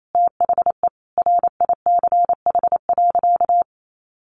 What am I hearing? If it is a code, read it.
Morse code, heard as a keyed tone, about 28 wpm, 700 Hz. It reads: THE RICH.
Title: Sentences from Top 1100 Words 28wpm